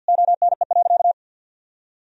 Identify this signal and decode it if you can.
Morse code, heard as a keyed tone, about 52 wpm, 700 Hz. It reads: ONE.
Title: Sentences from Top 1000 Words - Review Part 3 - 50wpm